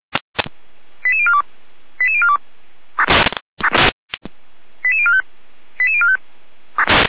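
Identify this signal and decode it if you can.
Five-tone selective call (ZVEI), heard as quick runs of five tones, tones of about 70 ms. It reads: 88042 88042 88045 88045
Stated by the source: ZVEI five-tone selective call received by SDR.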